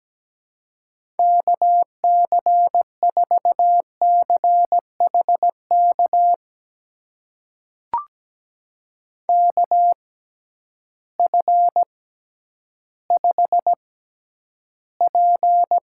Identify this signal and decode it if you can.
Morse code, heard as a keyed tone, about 17 wpm, spaced as for 10 wpm, 700 Hz. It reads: KC4CHK KF5P
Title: USA Call Signs - 6 Characters - 6x Character Spacing - 17wpm